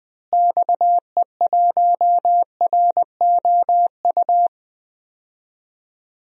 Morse code, heard as a keyed tone, about 20 wpm, 700 Hz. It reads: XE1ROU
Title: DXCC-100 Call Signs - 6 Characters - Repeated 3x - 20wpm